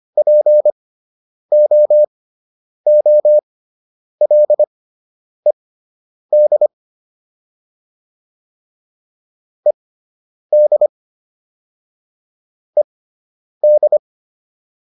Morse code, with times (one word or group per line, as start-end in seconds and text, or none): 0.2-6.7 POOLED
9.7-10.9 ED
12.8-14.0 ED